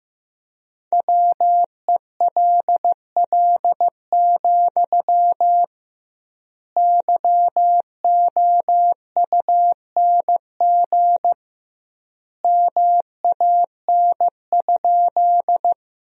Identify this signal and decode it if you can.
Morse code, heard as a keyed tone, about 15 wpm, 700 Hz. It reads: WELL, YOUNG MAN?
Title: 2x Word-Spacing - Sentences from Top 400 Words - 15wpm